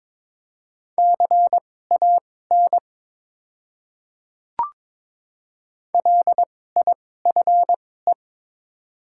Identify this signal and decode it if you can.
Morse code, heard as a keyed tone, about 22 wpm, spaced as for 16 wpm, 700 Hz. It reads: CAN LIFE
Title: 2x Character Spacing - Top 500 Words - 22wpm